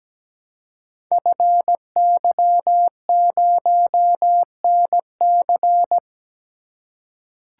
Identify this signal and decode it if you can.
Morse code, heard as a keyed tone, about 17 wpm, 700 Hz. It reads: FY0NC
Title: Code Groups - 5 Letter:Numbers - 17wpm